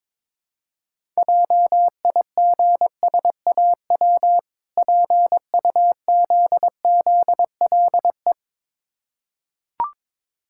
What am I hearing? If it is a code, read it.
Morse code, heard as a keyed tone, about 22 wpm, 690 Hz. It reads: JIGSAW PUZZLE
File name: Difficult Words with Hints - 22wpm